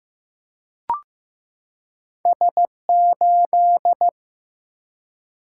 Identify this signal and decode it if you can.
Morse code, heard as a keyed tone, about 15 wpm, 700 Hz. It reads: S8
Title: Single-Letter Single-Number 15wpm